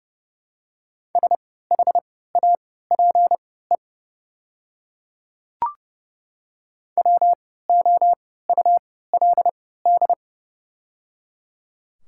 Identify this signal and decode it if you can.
Morse code, heard as a keyed tone, about 31 wpm, spaced as for 17 wpm, 700 Hz. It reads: SHAPE WOULD
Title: 3x Character Spacing - Top 500 Words - 30wpm